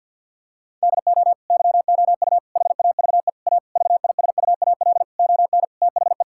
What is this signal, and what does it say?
Morse code, heard as a keyed tone, about 50 wpm, 700 Hz. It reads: DO YOU HAVE A VISUAL ON THE
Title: Sentences from Top 1000 Words - Review Part 6 - 50wpm